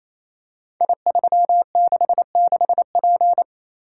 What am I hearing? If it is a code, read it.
Morse code, heard as a keyed tone, about 28 wpm, 700 Hz. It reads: I366P
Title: Code Groups - 5 Letter:Numbers - 28wpm